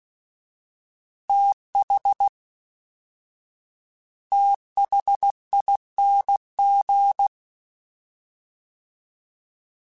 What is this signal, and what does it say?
Morse code, heard as a keyed tone, about 16 wpm, 800 Hz. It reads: TH THING
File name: TH / TH mp3s / TH Prefixes 3x2 @13